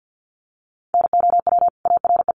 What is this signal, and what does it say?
Morse code, heard as a keyed tone, about 53 wpm, 700 Hz. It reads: NOW ARE
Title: Sentences from Top 900 Words 50wpm